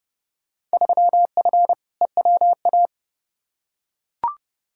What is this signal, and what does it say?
Morse code, heard as a keyed tone, about 30 wpm, 700 Hz. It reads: 3F EWA
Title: Field Day Exchanges Encore - 30wpm